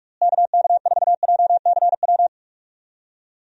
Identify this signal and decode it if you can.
Morse code, heard as a keyed tone, about 43 wpm, 700 Hz. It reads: KK4JCW
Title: USA Call Signs - 6 Characters - Repeated 2x - 45wpm